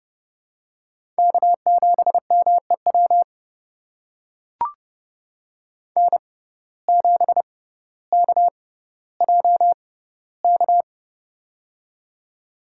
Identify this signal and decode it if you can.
Morse code, heard as a keyed tone, about 30 wpm, spaced as for 15 wpm, 700 Hz. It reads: K7MEW N7KJK
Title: USA Call Signs - 6x Character Spacing - 30wpm